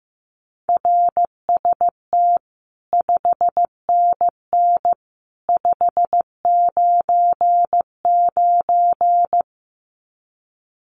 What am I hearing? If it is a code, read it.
Morse code, heard as a keyed tone, about 15 wpm, 700 Hz. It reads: RST 5NN 599
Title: Common QSO Phrases 15wpm